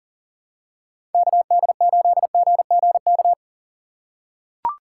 Morse code, heard as a keyed tone, about 40 wpm, 700 Hz. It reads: KD8GGK
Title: USA Call Signs - 6 Characters - 2x Character Spacing - 40wpm